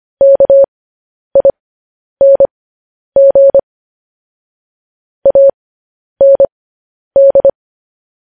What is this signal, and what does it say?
Morse code, heard as a keyed tone, about 25 wpm, spaced as for 10 wpm, 550 Hz. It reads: KING AND
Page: News Headlines in Morse Code at 10 WPM